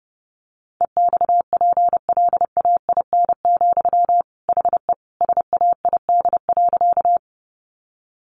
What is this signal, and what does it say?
Morse code, heard as a keyed tone, about 30 wpm, 700 Hz. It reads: EXPLAIN, HE SAID.
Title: Sentences from Top 1100 Words 30wpm